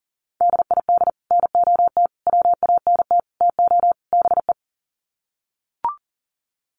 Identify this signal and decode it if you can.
Morse code, heard as a keyed tone, about 40 wpm, 700 Hz. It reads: DID NOT WANT TO BE
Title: Sets of 5 Words Encore 40wpm